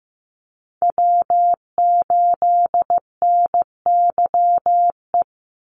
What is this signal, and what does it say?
Morse code, heard as a keyed tone, about 15 wpm, 700 Hz. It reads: W8NYE
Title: USA Call Signs 15wpm